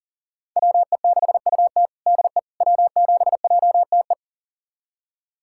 Morse code, heard as a keyed tone, about 37 wpm, 700 Hz. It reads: WE6UT DE W7JTE
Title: USA Callsign DE Callsign - 5 Characters - 40wpm